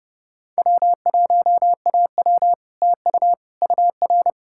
Morse code, heard as a keyed tone, about 37 wpm, 700 Hz. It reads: W1AW TU UR
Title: Speed-Racing Common QSO Phrases Encore - 30wpm